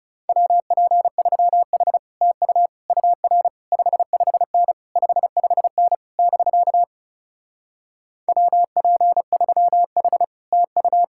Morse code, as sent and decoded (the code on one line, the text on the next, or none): .-- .--. ...-- .... / - ..- / ..- .-. / ..... ..... -. / ..... ..... -. / -...-.- / .-- .--. ...-- .... / - ..-
WP3H TU UR 55N 55N <BK> WP3H TU